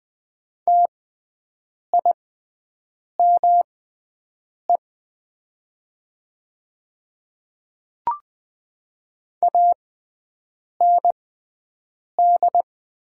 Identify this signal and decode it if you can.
Morse code, heard as a keyed tone, about 20 wpm, spaced as for 7 wpm, 700 Hz. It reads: TIME AND
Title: 6x Character Spacing - Top 100 Words - 20wpm